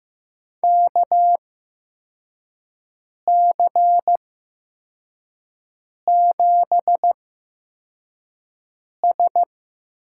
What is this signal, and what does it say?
Morse code, heard as a keyed tone, about 15 wpm, spaced as for 4 wpm, 700 Hz. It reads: KC7S
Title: USA Call Signs - 6 Characters - 8x Character Spacing - 15wpm